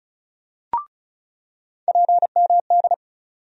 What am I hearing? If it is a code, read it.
Morse code, heard as a keyed tone, about 36 wpm, 700 Hz. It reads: PMD
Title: USA Call Sign Postfixes - 3 Letters 35wpm